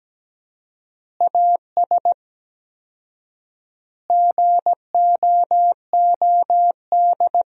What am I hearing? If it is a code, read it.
Morse code, heard as a keyed tone, about 17 wpm, 700 Hz. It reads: AS GOOD
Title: 4x Word-Spacing - Sentences from Top 200 Words - 17wpm